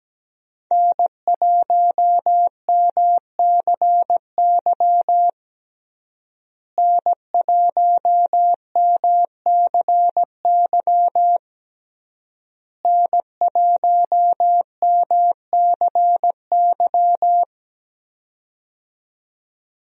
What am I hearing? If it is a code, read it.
Morse code, heard as a keyed tone, about 17 wpm, 700 Hz. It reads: N1MCY N1MCY N1MCY
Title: USA Call Signs - 5 Characters - Repeated 3x - 17wpm